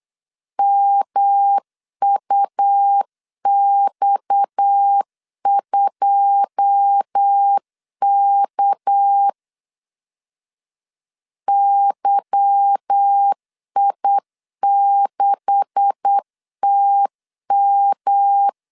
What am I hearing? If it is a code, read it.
Morse code, heard as a keyed tone, about 8 wpm, 800 Hz. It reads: MUX2K YI6TM
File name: CW / DET / 1 9 1500C S C 8wpm 5653